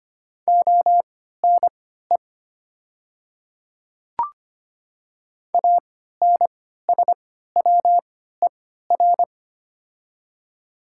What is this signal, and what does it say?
Morse code, heard as a keyed tone, about 25 wpm, spaced as for 14 wpm, 690 Hz. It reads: ONE ANSWER
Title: 3x Character Spacing - Top 300 Words - 25wpm